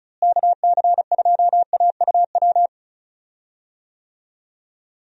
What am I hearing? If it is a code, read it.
Morse code, heard as a keyed tone, about 34 wpm, 700 Hz. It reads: KC2AUW
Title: USA Call Signs - 6 Characters - Repeated 2x - 35wpm